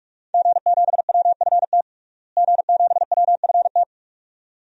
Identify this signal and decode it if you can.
Morse code, heard as a keyed tone, about 46 wpm, 700 Hz. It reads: G7WFT G7WFT
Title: DXCC-100 Call Signs - 5 Characters - Repeated 2x - 45wpm